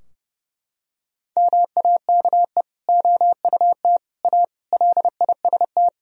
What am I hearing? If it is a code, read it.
Morse code, heard as a keyed tone, about 30 wpm, 700 Hz. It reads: MAKE OUT A LIST
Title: Sentences from Top 1000 Words - Review Part 2 - 30wpm